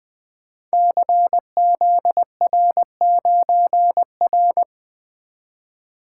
Morse code, heard as a keyed tone, about 20 wpm, 700 Hz. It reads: CZR9R
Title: Code Groups - 5 Letter:Numbers - 20wpm